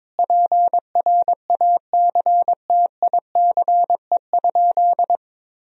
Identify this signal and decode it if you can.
Morse code, heard as a keyed tone, about 21 wpm, 700 Hz. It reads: PRACTICE?
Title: Sentences from Top 1000 Words - Review Part 4 - 22wpm